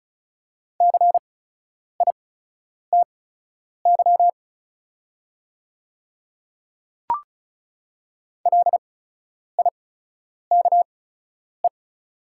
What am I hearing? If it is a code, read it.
Morse code, heard as a keyed tone, about 36 wpm, spaced as for 10 wpm, 690 Hz. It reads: CITY LIKE
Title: 8x Character Spacing - Top 500 Words - 35wpm